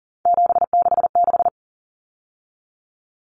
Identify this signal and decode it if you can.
Morse code, heard as a keyed tone, about 40 wpm, 700 Hz. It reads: 766
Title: Numbers - 3-Digit 40wpm